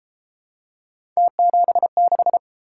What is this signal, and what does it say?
Morse code, heard as a keyed tone, about 29 wpm, 700 Hz. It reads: T76